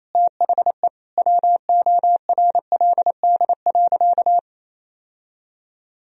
Morse code, heard as a keyed tone, about 28 wpm, 700 Hz. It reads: THE WORLD.
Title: Sentences from Top 1000 Words - Review Part 2 - 28wpm